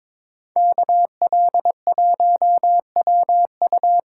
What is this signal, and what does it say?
Morse code, heard as a keyed tone, about 27 wpm, 700 Hz. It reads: KL1WU
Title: Speed Racing - USA 2x2 Call Signs 22wpm